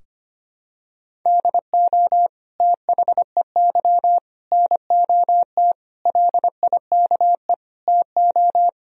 Morse code, heard as a keyed tone, about 25 wpm, 700 Hz. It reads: DO THEY NOT LIKE TO
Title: Sentences from Top 1000 Words - Review Part 4 - 25wpm